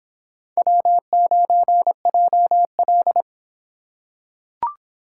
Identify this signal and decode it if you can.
Morse code, heard as a keyed tone, about 26 wpm, 700 Hz. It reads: W9JL